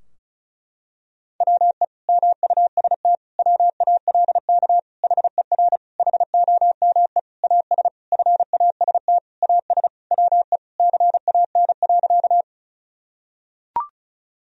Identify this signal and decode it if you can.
Morse code, heard as a keyed tone, about 34 wpm, 700 Hz. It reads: WE MUST WALK HER HOME AS FAST AS WE CAN.
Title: Sentences from Top 1000 Words - Review Part 2 - 35wpm